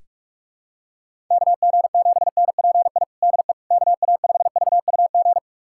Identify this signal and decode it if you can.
Morse code, heard as a keyed tone, about 46 wpm, 700 Hz. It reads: KG7NPI DE KA5VUG
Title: USA Callsign DE Callsign - 6 Characters - 45wpm